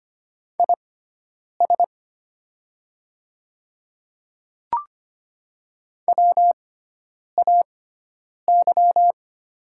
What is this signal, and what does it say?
Morse code, heard as a keyed tone, about 25 wpm, spaced as for 9 wpm, 700 Hz. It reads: IS WAY
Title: 6x Character Spacing - Top 100 Words - 25wpm